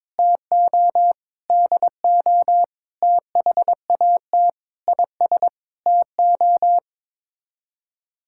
Morse code, heard as a keyed tone, about 22 wpm, 700 Hz. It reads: TO DO THAT IS TO
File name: practice / 22wpm / Sets of 8 Words - 22wpm